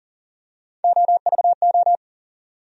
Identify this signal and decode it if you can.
Morse code, heard as a keyed tone, about 40 wpm, 700 Hz. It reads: OVO
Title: USA Call Sign Postfixes - 3 Letters 40wpm